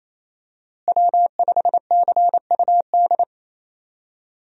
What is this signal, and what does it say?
Morse code, heard as a keyed tone, about 28 wpm, 700 Hz. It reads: W5CUD
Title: USA Call Signs - 5 Characters - Repeated 2x - 28wpm